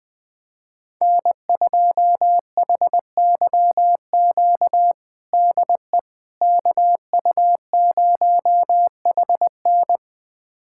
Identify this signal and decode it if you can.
Morse code, heard as a keyed tone, about 20 wpm, 700 Hz. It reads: N2HYQ DE KU0HN
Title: USA Callsign DE Callsign - 5 Characters - 20wpm